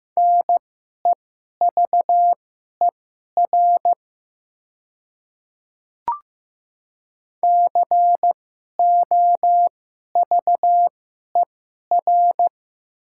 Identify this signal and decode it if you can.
Morse code, heard as a keyed tone, about 15 wpm, spaced as for 11 wpm, 690 Hz. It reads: NEVER COVER
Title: 2x Character Spacing - Top 500 Words - 15wpm